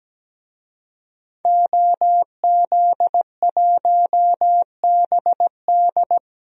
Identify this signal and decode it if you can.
Morse code, heard as a keyed tone, about 17 wpm, 700 Hz. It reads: OZ1BD
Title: DXCC-100 Call Signs - 5 Characters - Repeated 2x - 17wpm